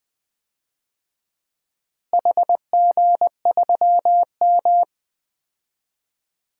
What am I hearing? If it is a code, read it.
Morse code, heard as a keyed tone, about 20 wpm, 700 Hz. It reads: HG3M
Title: DXCC-100 Call Signs - 4 Characters - Repeated 3x - 20wpm